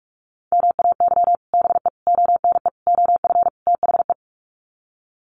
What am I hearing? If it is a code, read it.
Morse code, heard as a keyed tone, about 44 wpm, 690 Hz. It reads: MAY BE ONE OF THE
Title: Sets of 5 Words Encore2 45wpm